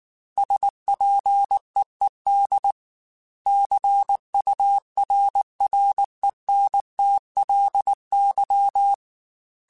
Morse code, heard as a keyed tone, about 19 wpm, 800 Hz. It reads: SPEED CURRENTLY
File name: Weather Sets / Weather Sets-mp3s / Weather 5 5 @18